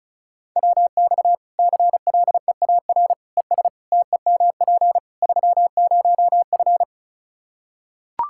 Morse code, heard as a keyed tone, about 36 wpm, 700 Hz. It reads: WX CLEAR ES TEMP 30F
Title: Practice QSO - Exchange 2 - Info - 35wpm